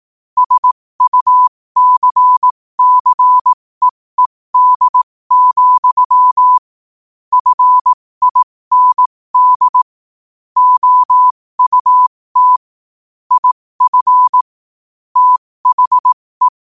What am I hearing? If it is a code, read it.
Morse code, heard as a keyed tone, about 18 wpm, spaced as for 15 wpm, 1000 Hz. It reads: SUCCEED, FIND OUT IF THE
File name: qotdfiles / 15wpm